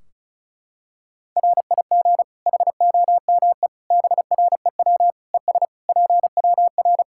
Morse code, heard as a keyed tone, about 34 wpm, 700 Hz. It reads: RIG HOME BREW ES PWR
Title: Practice QSO - Exchange 2 - Info - 35wpm